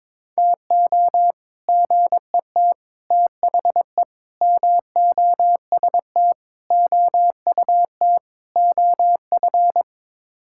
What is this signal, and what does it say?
Morse code, heard as a keyed tone, about 21 wpm, 700 Hz. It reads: TO GET THE MOST OUT OF
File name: practice / 22wpm / Sets of 6 Words - 22wpm